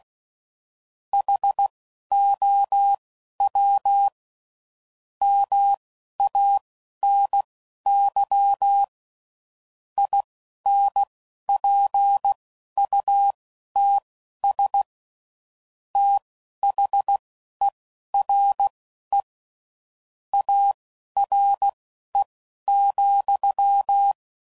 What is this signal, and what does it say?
Morse code, heard as a keyed tone, about 16 wpm, spaced as for 11 wpm, 800 Hz. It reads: HOW MANY INPUTS THERE ARE,
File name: CW Code Groups @10WPM/Technical Subjects